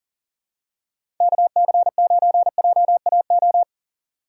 Morse code, heard as a keyed tone, about 40 wpm, 700 Hz. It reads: KC9JAO
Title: USA Call Signs - 6 Characters - 2x Character Spacing - 40wpm